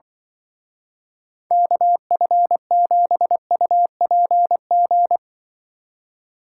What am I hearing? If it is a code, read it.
Morse code, heard as a keyed tone, about 23 wpm, 700 Hz. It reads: KF7UPG
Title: Speed Racing - USA 2x3 Call Signs 20wpm